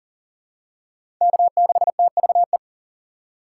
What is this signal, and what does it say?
Morse code, heard as a keyed tone, about 40 wpm, 700 Hz. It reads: K6TVE
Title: USA Call Signs - 3x Character Spacing - 40wpm